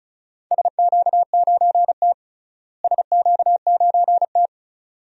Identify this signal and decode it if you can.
Morse code, heard as a keyed tone, about 36 wpm, 700 Hz. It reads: SQ9T SQ9T
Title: DXCC-100 Call Signs - 4 Characters - Repeated 2x - 35wpm